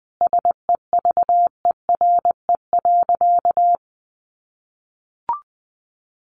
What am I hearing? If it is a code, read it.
Morse code, heard as a keyed tone, about 20 wpm, 700 Hz. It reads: SEVERE.
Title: Sentences from Top 1400 Words 20wpm